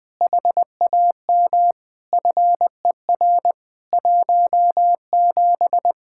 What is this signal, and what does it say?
Morse code, heard as a keyed tone, about 20 wpm, 700 Hz. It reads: HAM FER 17
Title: K1USN SST QSO Practice 20wpm